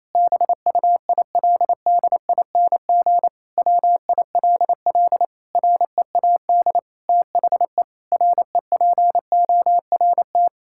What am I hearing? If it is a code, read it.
Morse code, heard as a keyed tone, about 28 wpm, 700 Hz. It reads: BUILDING WILL READ THE REPORT
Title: Sentences from Top 1000 Words - Review Part 6 - 28wpm